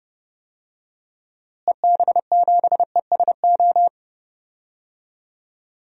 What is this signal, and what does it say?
Morse code, heard as a keyed tone, about 29 wpm, 700 Hz. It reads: EB7ESO
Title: DXCC-100 Call Signs - 6 Characters - Repeated 2x - 30wpm